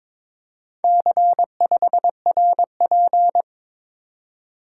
Morse code, heard as a keyed tone, about 23 wpm, 700 Hz. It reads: C5RP
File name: practice / 22wpm / Four Letter-Number Combinations - 22wpm